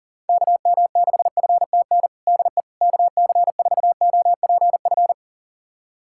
Morse code, heard as a keyed tone, about 40 wpm, 700 Hz. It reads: KM6FTN DE KC4OPF
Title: USA Callsign DE Callsign 40wpm